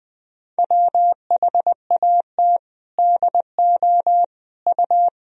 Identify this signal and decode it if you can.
Morse code, heard as a keyed tone, about 21 wpm, 700 Hz. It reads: WHAT DO U
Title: Sentences from Top 200 Words and Abbreviations 20wpm